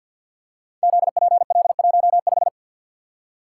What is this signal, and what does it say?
Morse code, heard as a keyed tone, about 49 wpm, 700 Hz. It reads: ZPL15
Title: Code Groups - 5 Letter:Numbers - 50wpm